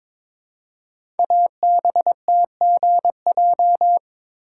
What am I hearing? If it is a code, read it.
Morse code, heard as a keyed tone, about 22 wpm, 700 Hz. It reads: ABTGJ
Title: Code Groups - 5 Letters - 22wpm